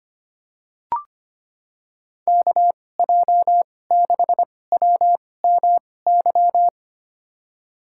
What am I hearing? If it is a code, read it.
Morse code, heard as a keyed tone, about 25 wpm, spaced as for 18 wpm, 700 Hz. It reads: KJ6WMY
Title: USA Call Signs - 6 Characters - 2x Character Spacing - 25wpm